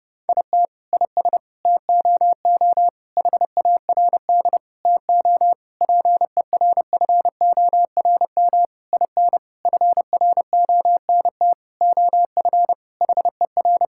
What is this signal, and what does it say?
Morse code, heard as a keyed tone, about 30 wpm, 700 Hz. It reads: IT IS TOO HARD TO PERFORM IN FRONT OF HER
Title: Sentences from Top 1000 Words - Review Part 5 - 30wpm